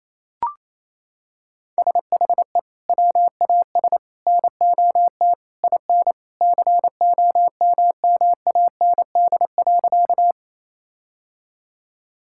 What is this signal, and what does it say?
Morse code, heard as a keyed tone, about 27 wpm, 700 Hz. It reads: SHE WAS NOT IN COMMAND.
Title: Sentences from Top 1600 Words 28wpm